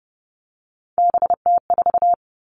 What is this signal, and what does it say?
Morse code, heard as a keyed tone, about 30 wpm, 700 Hz. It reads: BT4
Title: All Cut Numbers - 3-Digit 30wpm